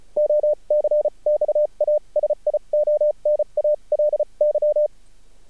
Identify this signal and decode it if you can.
Morse code, heard as a keyed tone, about 34 wpm, spaced as for 29 wpm, 600 Hz. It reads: OCXASIONALY
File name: archived / CRSnet